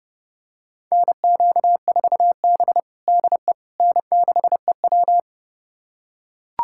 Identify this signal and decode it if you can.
Morse code, heard as a keyed tone, about 32 wpm, 700 Hz. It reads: NQ4B DE N6EW
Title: USA Callsign DE Callsign - 4 Characters - 30wpm